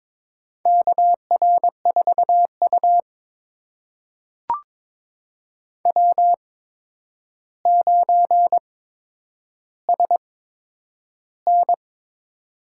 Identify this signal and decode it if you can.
Morse code, heard as a keyed tone, about 22 wpm, spaced as for 9 wpm, 700 Hz. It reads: KR4U W9SN
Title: USA Call Signs - 4 Characters - 8x Character Spacing - 22wpm